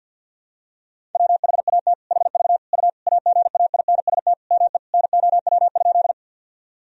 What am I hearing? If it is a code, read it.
Morse code, heard as a keyed tone, about 52 wpm, 700 Hz. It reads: WHAT HV U AGAINST ME NOW?
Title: Sentences from Top 400 Words and Abbreviations 50wpm